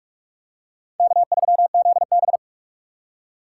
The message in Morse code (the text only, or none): K3ZB